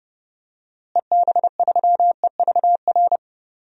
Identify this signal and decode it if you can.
Morse code, heard as a keyed tone, about 30 wpm, 700 Hz. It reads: EB3EVR
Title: DXCC-100 Call Signs - 6 Characters - Repeated 3x - 30wpm